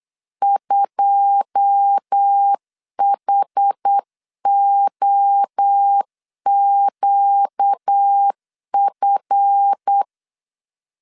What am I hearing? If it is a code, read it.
Morse code, heard as a keyed tone, about 8 wpm, 800 Hz. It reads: 2HOQF
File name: CW / DET / 1 9 1500C S A 8wpm 5637